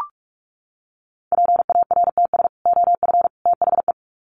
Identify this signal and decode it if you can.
Morse code, heard as a keyed tone, about 44 wpm, 700 Hz. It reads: PARTS OF THE
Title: Sets of 3 Words 45wpm